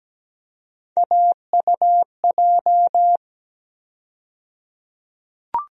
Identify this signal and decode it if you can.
Morse code, heard as a keyed tone, about 17 wpm, 700 Hz. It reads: AUJ
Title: USA Call Sign Postfixes - 3 Letters 17wpm